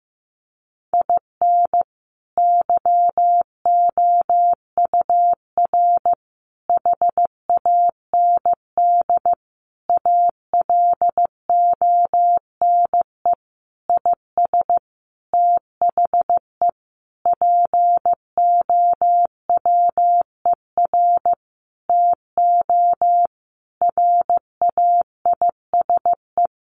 Morse code, heard as a keyed tone, about 15 wpm, 700 Hz. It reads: IN YOUR HAND ALONE IS THE POWER TO RAISE
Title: Sentences from Top 1400 Words 15wpm